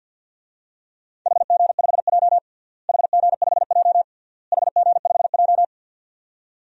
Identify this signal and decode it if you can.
Morse code, heard as a keyed tone, about 47 wpm, 700 Hz. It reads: HG5J HG5J HG5J
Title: DXCC-100 Call Signs - 4 Characters - Repeated 3x - 50wpm